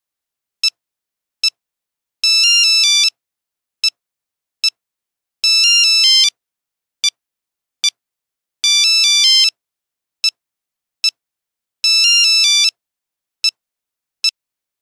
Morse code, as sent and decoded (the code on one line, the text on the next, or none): none